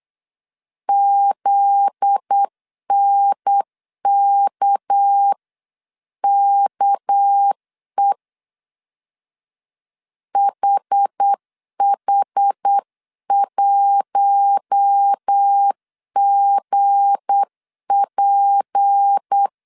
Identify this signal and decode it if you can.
Morse code, heard as a keyed tone, about 8 wpm, 800 Hz. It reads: ZNKKE HH1GP
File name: CW / DET / 1 9 1500C S A 8wpm 5637